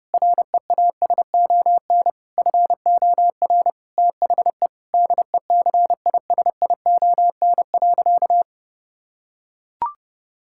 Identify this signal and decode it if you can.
Morse code, heard as a keyed tone, about 30 wpm, 700 Hz. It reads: REASON FOR THE DECISION.
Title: Sentences from Top 1600 Words 30wpm